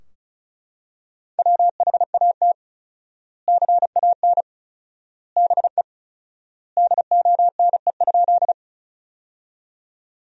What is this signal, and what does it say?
Morse code, heard as a keyed tone, about 36 wpm, 700 Hz. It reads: WHAT CAN BE DONE?
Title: 4x Word-Spacing - Sentences from Top 500 Words - 35wpm